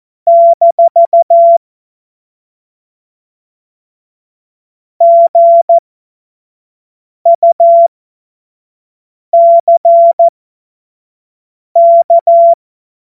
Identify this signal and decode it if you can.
Morse code, heard as a keyed tone, about 14 wpm, spaced as for 5 wpm, 680 Hz. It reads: - GUCK
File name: Morsekurs / Witze / Witz 28. März 2023 25 F70